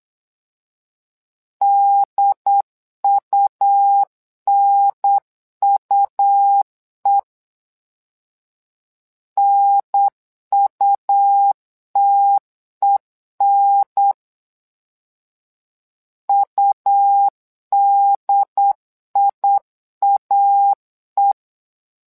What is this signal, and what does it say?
Morse code, heard as a keyed tone, about 8 wpm, 800 Hz. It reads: DUNUE NUTEN UDIAE
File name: LEZIONI / 2 UHDN / 2 UHDN 8wpm2232